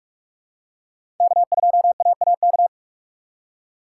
Morse code, heard as a keyed tone, about 46 wpm, 700 Hz. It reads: K2AAK
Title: USA Call Signs - 5 Characters - 4x Character Spacing - 45wpm